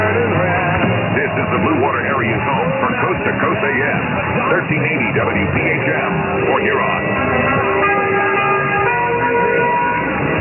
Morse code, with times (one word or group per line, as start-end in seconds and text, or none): none